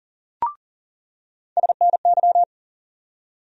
Morse code, heard as a keyed tone, about 40 wpm, 700 Hz. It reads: SNY
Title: USA Call Sign Postfixes - 3 Letters 40wpm